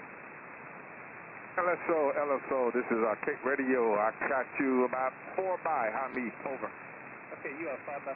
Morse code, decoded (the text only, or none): none